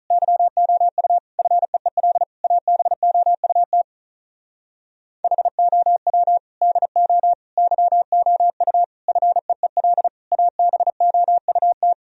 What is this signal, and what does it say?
Morse code, heard as a keyed tone, about 35 wpm, 700 Hz. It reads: YOU FEEL ABOUT HOW DO YOU FEEL ABOUT
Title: Speed-Racing - Sets of 5 Words 35wpm